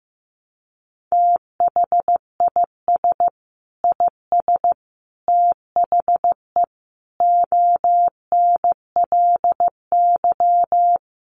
Morse code, heard as a keyed tone, about 15 wpm, 700 Hz. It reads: THIS IS THE ONLY
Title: Sentences from Top 1400 Words 15wpm